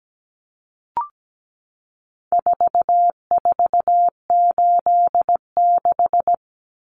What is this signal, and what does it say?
Morse code, heard as a keyed tone, about 17 wpm, 700 Hz. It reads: 4486
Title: Numbers - 4-Digit 17wpm